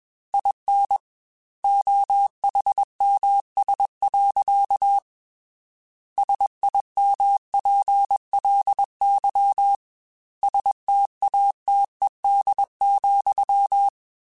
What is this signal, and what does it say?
Morse code, heard as a keyed tone, about 21 wpm, 800 Hz. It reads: IN OHMS. SIMPLY STATED,
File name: Technical Subjects / Technical Subjects-mp3s / XL @20